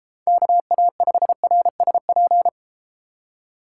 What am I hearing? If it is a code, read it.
Morse code, heard as a keyed tone, about 33 wpm, 700 Hz. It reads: KA5RSP